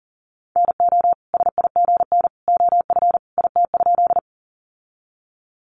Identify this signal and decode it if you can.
Morse code, heard as a keyed tone, about 40 wpm, 700 Hz. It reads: NO SIGN OF IT?
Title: Sentences from Top 1000 Words 40wpm